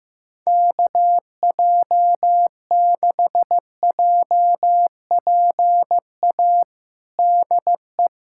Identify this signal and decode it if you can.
Morse code, heard as a keyed tone, about 15 wpm, 700 Hz. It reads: KJ6JPA DE
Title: USA Callsign DE Callsign - 6 Characters - 15wpm